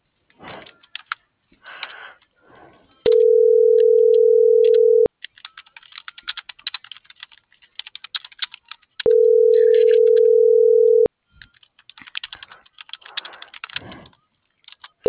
Ringback tone (North America), 440 + 480 Hz.